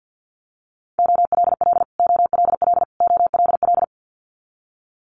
Morse code, heard as a keyed tone, about 50 wpm, 690 Hz. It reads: OLL OLL OLL
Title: Multi-letter Phonemes Repeated 3x 50wpm